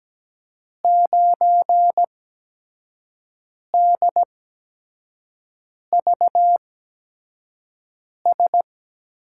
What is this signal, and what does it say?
Morse code, heard as a keyed tone, about 17 wpm, 700 Hz. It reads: 9 D V S
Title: USA Call Signs - 5 Characters - 8x Character Spacing - 17wpm